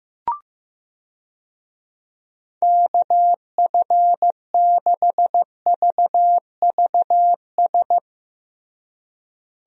Morse code, heard as a keyed tone, about 15 wpm, 700 Hz. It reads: KF6VVS
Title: USA Call Signs - 6 Characters - Repeated 3x - 15wpm